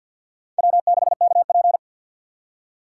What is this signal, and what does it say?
Morse code, heard as a keyed tone, about 52 wpm, 700 Hz. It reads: W6KP
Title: USA Call Signs - 6x Character Spacing - 50wpm